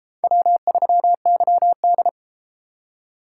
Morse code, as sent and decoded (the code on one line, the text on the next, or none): .-- ...-- -.-- -..
W3YD